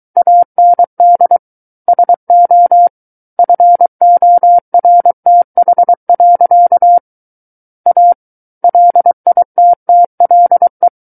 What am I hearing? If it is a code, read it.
Morse code, heard as a keyed tone, about 23 wpm, 700 Hz. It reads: AND SO FORTH. A LITTLE
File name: Book One / 14-Chapter 14 - 23 WPM